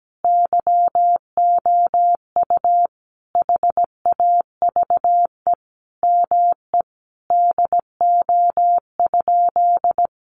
Morse code, heard as a keyed tone, about 17 wpm, 700 Hz. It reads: YOU HAVE ME DO?
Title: Sentences from Top 200 Words 17wpm